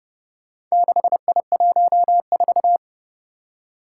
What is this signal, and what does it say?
Morse code, heard as a keyed tone, about 30 wpm, 700 Hz. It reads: 6I14
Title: Four Challenging Letter-Number Combinations 30wpm